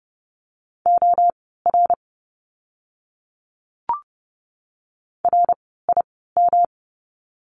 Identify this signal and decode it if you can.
Morse code, heard as a keyed tone, about 30 wpm, spaced as for 17 wpm, 700 Hz. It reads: OR RIM
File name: course / 30:10wpm / 108 Lesson 18f - Letter m - Words 30wpm